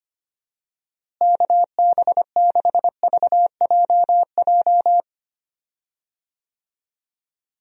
Morse code, heard as a keyed tone, about 25 wpm, 700 Hz. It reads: KB6VJJ
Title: USA Call Signs - 6 Characters - Repeated 3x - 25wpm